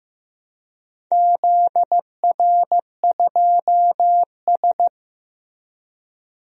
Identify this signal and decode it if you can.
Morse code, heard as a keyed tone, about 15 wpm, 700 Hz. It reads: ZR2S
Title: DXCC-100 Call Signs - 4 Characters - 15wpm